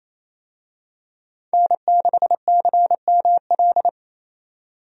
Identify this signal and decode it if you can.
Morse code, heard as a keyed tone, about 28 wpm, 700 Hz. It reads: N6CML
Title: USA Call Signs - 5 Characters - 28wpm